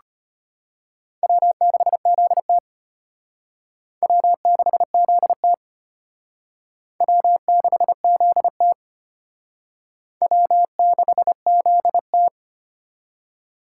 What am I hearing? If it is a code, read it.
Morse code, heard as a keyed tone, about 34 wpm, 700 Hz. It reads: W6ZT W6ZT W6ZT W6ZT